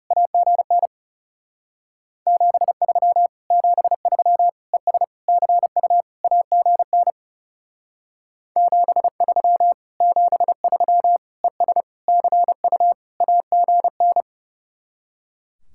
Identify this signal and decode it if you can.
Morse code, heard as a keyed tone, about 35 wpm, 700 Hz. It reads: AGN 73 73 ES CU AGN 73 73 ES CU AGN
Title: Speed-Racing Common QSO Phrases - 30wpm